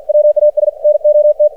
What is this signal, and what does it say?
Morse code, heard as a keyed tone, about 53 wpm, 600 Hz. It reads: WAS NOT
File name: archived / CRSnet